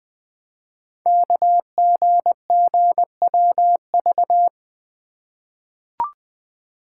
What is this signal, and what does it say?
Morse code, heard as a keyed tone, about 20 wpm, 700 Hz. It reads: KGGWV
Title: Code Groups - 5 Letters - 20wpm